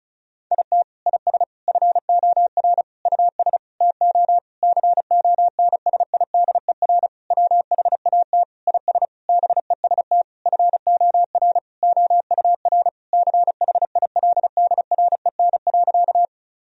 Morse code, heard as a keyed tone, about 36 wpm, 700 Hz. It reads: IT IS FOR US TO CONSIDER WHAT IS BEST FOR OUR CHILDREN.
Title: Sentences from Top 1000 Words - Review Part 4 - 35wpm